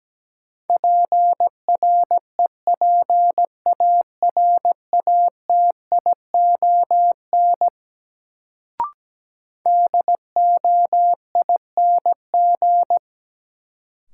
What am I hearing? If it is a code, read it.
Morse code, heard as a keyed tone, about 17 wpm, 700 Hz. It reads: PREPARATION DOING
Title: Sentences from Top 1400 Words - New Word Drill Alt - 17wpm